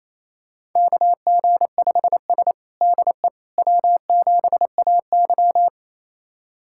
Morse code, heard as a keyed tone, about 28 wpm, 700 Hz. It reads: KG5S DE W7AY
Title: USA Callsign DE Callsign - 4 Characters - 28wpm